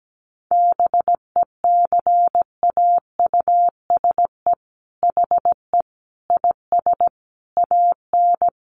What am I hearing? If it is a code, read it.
Morse code, heard as a keyed tone, about 17 wpm, 700 Hz. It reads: BECAUSE HE IS AN
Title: Sentences from Top 1500 Words 17wpm